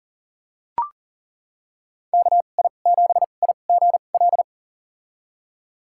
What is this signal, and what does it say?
Morse code, heard as a keyed tone, about 40 wpm, spaced as for 29 wpm, 700 Hz. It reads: KI7IGL
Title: USA Call Signs - 2x Character Spacing - 40wpm